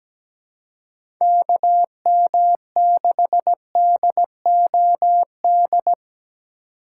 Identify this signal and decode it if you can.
Morse code, heard as a keyed tone, about 17 wpm, 700 Hz. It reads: KM6DOD
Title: USA Call Signs - 6 Characters - 6x Character Spacing - 17wpm